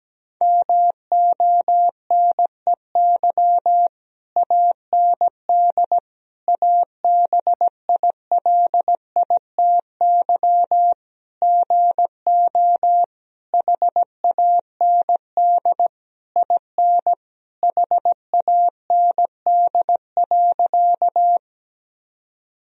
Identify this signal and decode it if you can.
Morse code, heard as a keyed tone, about 17 wpm, 700 Hz. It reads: MONEY AND ABILITY GO HAND IN HAND.
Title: Sentences from Top 1000 Words - Review Part 3 - 17wpm